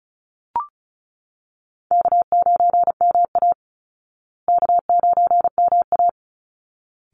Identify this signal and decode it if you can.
Morse code, heard as a keyed tone, about 34 wpm, 700 Hz. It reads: K9MA K9MA
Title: Call Signs Repeated 3x - 35wpm